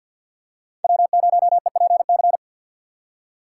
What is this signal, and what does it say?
Morse code, heard as a keyed tone, about 47 wpm, 700 Hz. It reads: W0EPC
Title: K1USN SST QSO Practice - 50wpm